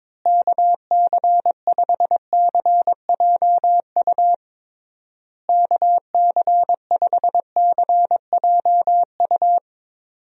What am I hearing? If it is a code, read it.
Morse code, heard as a keyed tone, about 22 wpm, 700 Hz. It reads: KC5CJU KC5CJU
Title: USA Call Signs - 6 Characters - Repeated 3x - 22wpm